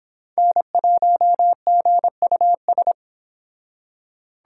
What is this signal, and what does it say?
Morse code, heard as a keyed tone, about 29 wpm, 700 Hz. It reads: N1GUS